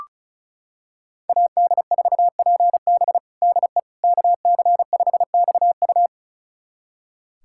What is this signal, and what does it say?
Morse code, heard as a keyed tone, about 34 wpm, 690 Hz. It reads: AD4PB DE KC5XU
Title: USA Callsign DE Callsign - 5 Characters - 35wpm